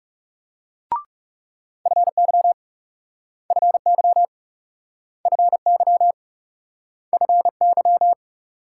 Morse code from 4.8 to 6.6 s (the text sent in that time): FY